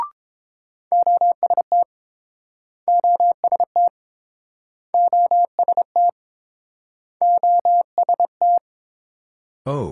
Morse code, heard as a keyed tone, about 31 wpm, 700 Hz. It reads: OST OST OST OST